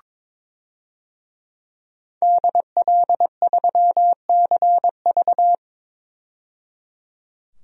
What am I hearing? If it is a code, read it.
Morse code, heard as a keyed tone, about 21 wpm, 700 Hz. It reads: DL3CV
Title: DXCC-100 Call Signs - 5 Characters - 22wpm